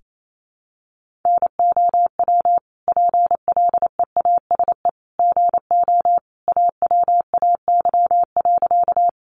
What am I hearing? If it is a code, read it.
Morse code, heard as a keyed tone, about 28 wpm, 700 Hz. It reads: NOW PLEASE GO AWAY.
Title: Sentences from Top 1500 Words 28wpm